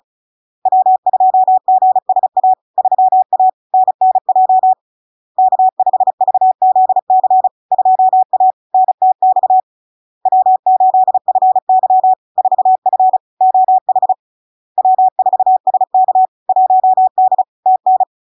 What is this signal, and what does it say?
Morse code, heard as a keyed tone, about 36 wpm, 750 Hz. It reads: W2GSA 3A NNJ K5VZC 2A NTX W8FY 4F OH W4SK 1D TN
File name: LARK-FD-2022-Log-in-CW-35wpm-750-Hz